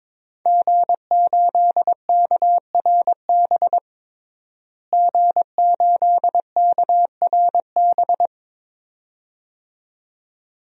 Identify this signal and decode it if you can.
Morse code, heard as a keyed tone, about 22 wpm, 700 Hz. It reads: G8KRB G8KRB
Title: DXCC-100 Call Signs - 5 Characters - Repeated 2x - 22wpm